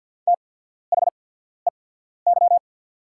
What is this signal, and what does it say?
Morse code, heard as a keyed tone, about 48 wpm, spaced as for 14 wpm, 700 Hz. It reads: THEY
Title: 8x Character Spacing - Top 200 Words - 50wpm